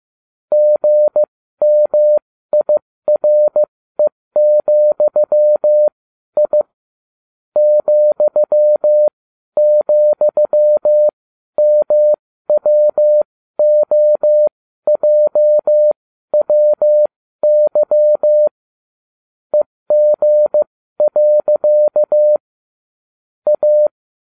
Morse code, heard as a keyed tone, about 15 wpm, spaced as for 12 wpm, 600 Hz. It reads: GMIRE,I ,,MWOJWY EG. A